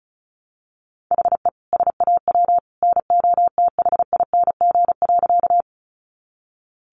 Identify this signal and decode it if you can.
Morse code, heard as a keyed tone, about 34 wpm, 700 Hz. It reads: HE SAW NOTHING.
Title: Sentences from Top 500 Words 35wpm